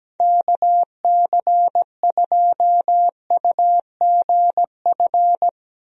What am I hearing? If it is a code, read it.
Morse code, heard as a keyed tone, about 17 wpm, 700 Hz. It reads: KC2UGF
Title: USA Call Signs - 6 Characters - Repeated 2x - 17wpm